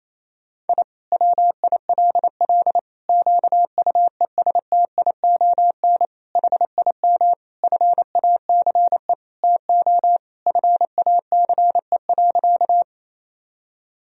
Morse code, heard as a keyed tone, about 28 wpm, 700 Hz. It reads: I WILL QUESTION HIM FACE TO FACE.
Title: Sentences from Top 1000 Words - Review Part 2 - 28wpm